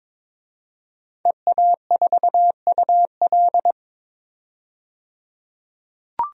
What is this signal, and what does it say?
Morse code, heard as a keyed tone, about 21 wpm, 700 Hz. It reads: EA4UL
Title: DXCC-100 Call Signs - 5 Characters - 22wpm